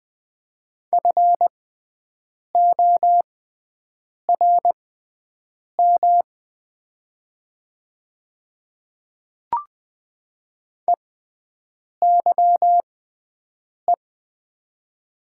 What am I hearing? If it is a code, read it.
Morse code, heard as a keyed tone, about 20 wpm, spaced as for 7 wpm, 700 Hz. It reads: FORM EYE
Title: 6x Character Spacing - Top 500 Words - 20wpm